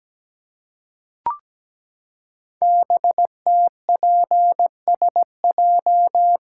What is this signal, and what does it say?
Morse code, heard as a keyed tone, about 17 wpm, 700 Hz. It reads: BTPSJ
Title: Code Groups - 5 Letters - 17wpm